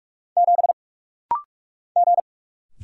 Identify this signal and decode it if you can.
Morse code, heard as a keyed tone, about 46 wpm, 700 Hz. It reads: 7 G
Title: Single Letter-Number Alt - Rapid-Fire 45wpm